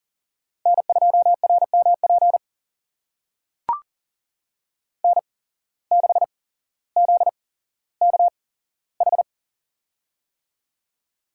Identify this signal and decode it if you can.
Morse code, heard as a keyed tone, about 39 wpm, spaced as for 17 wpm, 700 Hz. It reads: N2RMP N6ZKH
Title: USA Call Signs - 5 Characters - 8x Character Spacing - 40wpm